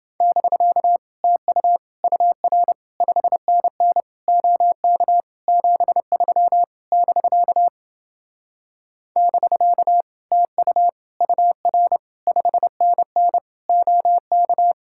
Morse code, heard as a keyed tone, about 28 wpm, 700 Hz. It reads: <BK> TU UR 5NN OK 73 <BK> <BK> TU UR 5NN OK